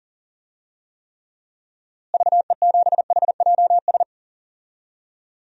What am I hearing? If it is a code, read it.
Morse code, heard as a keyed tone, about 40 wpm, 700 Hz. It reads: VE7HJS
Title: DXCC-100 Call Signs - 6 Characters - Repeated 2x - 40wpm